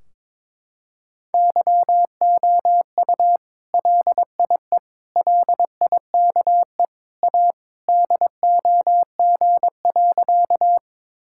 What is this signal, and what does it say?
Morse code, heard as a keyed tone, about 22 wpm, 700 Hz. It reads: YOU LIE LIKE A DOG.
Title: Sentences from Top 1600 Words 22wpm